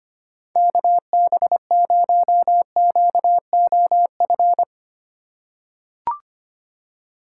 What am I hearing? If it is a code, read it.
Morse code, heard as a keyed tone, about 25 wpm, 700 Hz. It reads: KB0QOF
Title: USA Call Signs - 6 Characters - 4x Character Spacing - 25wpm